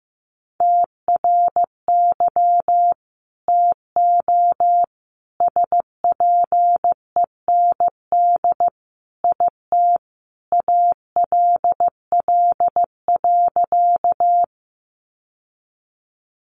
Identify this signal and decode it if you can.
Morse code, heard as a keyed tone, about 15 wpm, 700 Hz. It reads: TRY TO SPEND IT ALL.
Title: Sentences from Top 800 Words 15wpm